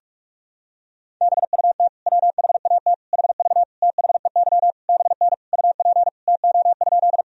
Morse code, heard as a keyed tone, about 42 wpm, 700 Hz. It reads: BUT WHAT HV THEY BN UP TO?
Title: Sentences from Top 100 Words and Abbreviations 45wpm